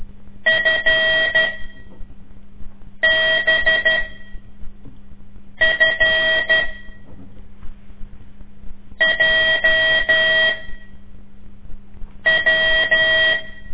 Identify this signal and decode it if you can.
Morse code, heard as a keyed tone, about 12 wpm, spaced as for 4 wpm, 1900 Hz. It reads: FBFJW
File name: ik1weg2 / CORSI CW / lez5-2